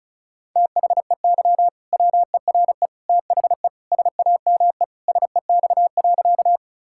Morse code, heard as a keyed tone, about 34 wpm, 700 Hz. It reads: THEY WERE THE SAME SEX.
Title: Sentences from Top 1600 Words 35wpm